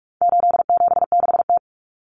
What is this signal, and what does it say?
Morse code, heard as a keyed tone, about 44 wpm, 700 Hz. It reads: 876T
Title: Common Cut Numbers - 4-Digit 45wpm